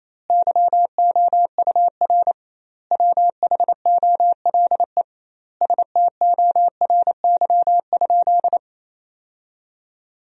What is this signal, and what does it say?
Morse code, heard as a keyed tone, about 28 wpm, 700 Hz. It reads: YOUR WHOLE STORY?
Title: 2x Word-Spacing - Sentences from Top 400 Words - 28wpm